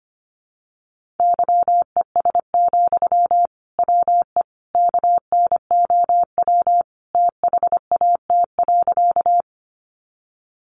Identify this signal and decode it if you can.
Morse code, heard as a keyed tone, about 25 wpm, 700 Hz. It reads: YES, WE KNOW THAT.
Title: Sentences from Top 500 Words 25wpm